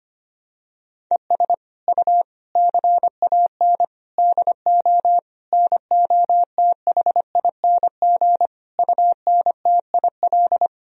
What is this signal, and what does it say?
Morse code, heard as a keyed tone, about 25 wpm, 700 Hz. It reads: ES U CAN DO NOTHING UNTIL
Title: Sentences from Top 500 Words and Abbreviations 25wpm